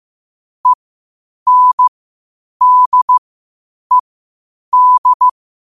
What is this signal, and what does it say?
Morse code, heard as a keyed tone, about 15 wpm, spaced as for 8 wpm, 1000 Hz. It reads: ENDED
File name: qotdfiles / 8wpm